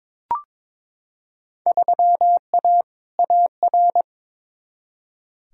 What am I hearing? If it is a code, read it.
Morse code, heard as a keyed tone, about 22 wpm, 700 Hz. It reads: 3A AR
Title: Field Day Exchanges Encore - 22wpm